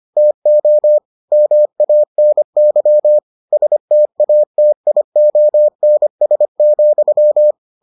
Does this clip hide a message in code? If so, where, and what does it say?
Yes: Morse code from 0.2 to 7.5 s, heard as a keyed tone, about 25 wpm, 600 Hz. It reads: TO MANY STATIONS,